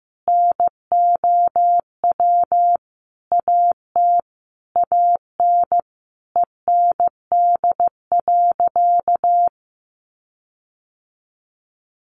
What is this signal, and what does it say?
Morse code, heard as a keyed tone, about 15 wpm, 700 Hz. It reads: NOW AT AN END.
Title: Sentences from Top 1500 Words 15wpm